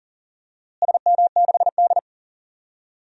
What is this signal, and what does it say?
Morse code, heard as a keyed tone, about 40 wpm, 700 Hz. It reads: SM6D
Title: DXCC-100 Call Signs - 4 Characters - Repeated 2x - 40wpm